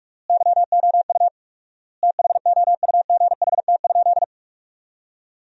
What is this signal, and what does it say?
Morse code, heard as a keyed tone, about 43 wpm, 700 Hz. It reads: YOU THOUGHT?
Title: 4x Word-Spacing - Sentences from Top 300 Words - 45wpm